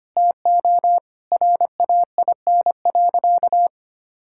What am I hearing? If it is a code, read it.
Morse code, heard as a keyed tone, about 25 wpm, 700 Hz. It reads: TO RAIN.
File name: practice / 25wpm / Sentences with 2-Word Phrases - 25wpm